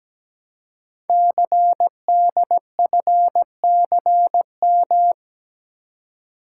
Morse code, heard as a keyed tone, about 17 wpm, 700 Hz. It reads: CDFCM
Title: Code Groups - 5 Letters - 17wpm